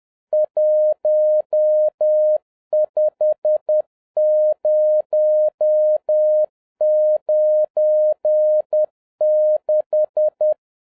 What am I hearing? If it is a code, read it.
Morse code, heard as a keyed tone, about 10 wpm, 610 Hz. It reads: 15096